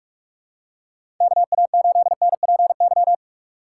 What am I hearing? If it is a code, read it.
Morse code, heard as a keyed tone, about 45 wpm, 700 Hz. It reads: KA8NPY
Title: USA Call Signs - 4x Character Spacing - 45wpm